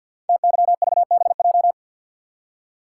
Morse code, heard as a keyed tone, about 52 wpm, 700 Hz. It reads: TYVBJ
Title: Code Groups - 5 Letters - 50wpm